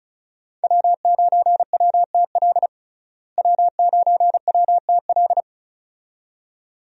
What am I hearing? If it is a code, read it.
Morse code, heard as a keyed tone, about 36 wpm, 700 Hz. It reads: W9WTL W9WTL
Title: USA Call Signs - 5 Characters - Repeated 2x - 35wpm